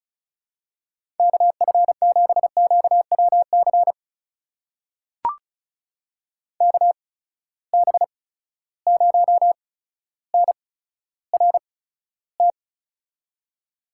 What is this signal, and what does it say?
Morse code, heard as a keyed tone, about 36 wpm, spaced as for 15 wpm, 700 Hz. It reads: KF7QWC KB0NRT
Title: USA Call Signs - 6 Characters - 8x Character Spacing - 35wpm